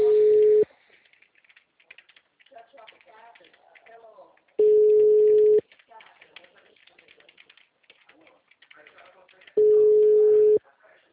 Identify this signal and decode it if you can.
Ringback tone (Europe, 425 Hz), with speech.
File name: out- -2015- - - .5597